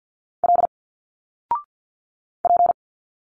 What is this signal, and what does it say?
Morse code, heard as a keyed tone, about 50 wpm, 700 Hz. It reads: L P
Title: Single Letters 50wpm